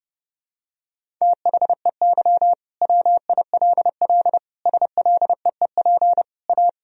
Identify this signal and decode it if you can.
Morse code, heard as a keyed tone, about 32 wpm, 700 Hz. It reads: THEY WILL SLEEP A
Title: Sentences from Top 1000 Words - Review Part 2 - 30wpm